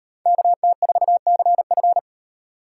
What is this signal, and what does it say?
Morse code, heard as a keyed tone, about 36 wpm, 700 Hz. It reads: KT4CF